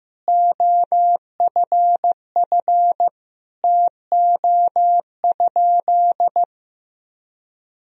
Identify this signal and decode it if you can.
Morse code, heard as a keyed tone, about 16 wpm, 700 Hz. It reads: OFF TO?
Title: Sentences from Top 200 Words and Abbreviations 15wpm